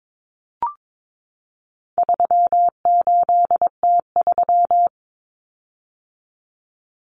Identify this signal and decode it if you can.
Morse code, heard as a keyed tone, about 22 wpm, 700 Hz. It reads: 38T3
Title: Common Cut Numbers - 4-Digit 22wpm